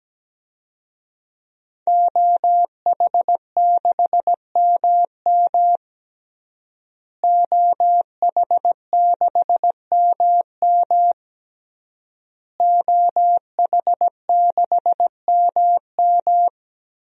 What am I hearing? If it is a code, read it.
Morse code, heard as a keyed tone, about 17 wpm, 700 Hz. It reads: OH6MM OH6MM OH6MM
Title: DXCC-100 Call Signs - 5 Characters - Repeated 3x - 17wpm